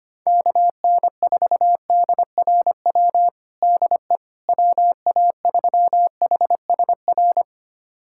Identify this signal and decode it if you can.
Morse code, heard as a keyed tone, about 25 wpm, 700 Hz. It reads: KN4DRW DE WA3HSR
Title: USA Callsign DE Callsign 25wpm